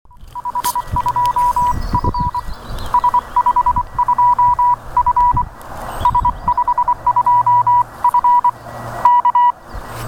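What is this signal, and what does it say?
Morse code, heard as a keyed tone, about 24 wpm, 980 Hz. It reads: H2F S52F S52F K
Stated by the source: Short Morse recording of S52F showing how much propagation had improved at the time, shared by UA1CEG in the RU-QRP Club thread on QSOs with homemade rigs.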